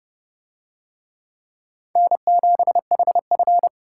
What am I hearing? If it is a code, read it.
Morse code, heard as a keyed tone, about 29 wpm, 700 Hz. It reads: N7HF